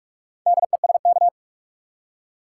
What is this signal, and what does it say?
Morse code, heard as a keyed tone, about 42 wpm, 700 Hz. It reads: DESK
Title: Four Letter Words - 45wpm